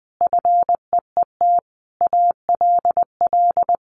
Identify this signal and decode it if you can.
Morse code, heard as a keyed tone, about 20 wpm, 700 Hz. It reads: FEET ALL
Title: Sentences from Top 1500 Words 20wpm